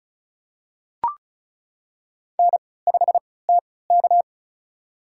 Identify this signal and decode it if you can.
Morse code, heard as a keyed tone, about 36 wpm, spaced as for 20 wpm, 700 Hz. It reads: N5TK
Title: USA Call Signs - 4 Characters - 3x Character Spacing - 35wpm